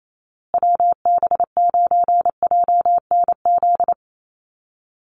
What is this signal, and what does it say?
Morse code, heard as a keyed tone, about 28 wpm, 700 Hz. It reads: WB9JNZ
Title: USA Call Signs Repeated 2x 28wpm